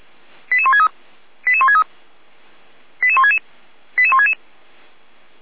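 Five-tone selective call (ZVEI), heard as quick runs of five tones, tones of about 70 ms. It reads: 88163 88163 88166 88166
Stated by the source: ZVEI five-tone selective call 88207, received by SDR, from audio.f85360.feuerwehr-orlamuende.org fire brigade archive.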